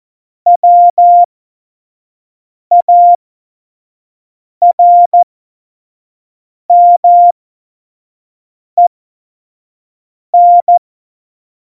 Morse code, heard as a keyed tone, about 14 wpm, spaced as for 5 wpm, 700 Hz. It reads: WARMEN
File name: Morsekurs / Witze / Witz 20. August 2019, 25 F70